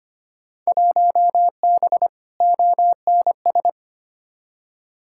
Morse code, heard as a keyed tone, about 25 wpm, 700 Hz. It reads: 1B ONS
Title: Field Day Exchanges Encore - 25wpm